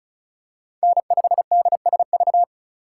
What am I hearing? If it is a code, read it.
Morse code, heard as a keyed tone, about 36 wpm, 700 Hz. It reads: N5DSV
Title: USA Call Signs - 5 Characters - 3x Character Spacing - 35wpm